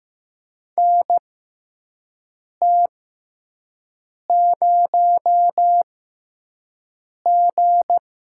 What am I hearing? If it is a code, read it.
Morse code, heard as a keyed tone, about 15 wpm, spaced as for 5 wpm, 700 Hz. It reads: NT0G
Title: USA Call Signs - 4 Characters - 6x Character Spacing - 15wpm